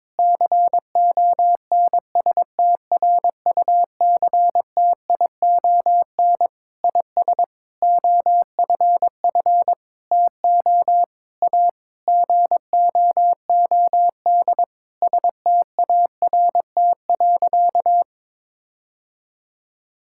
Morse code, heard as a keyed tone, about 22 wpm, 700 Hz. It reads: CONSTRUCTION IS OFF TO A GOOD START.
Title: Sentences from Top 1600 Words 22wpm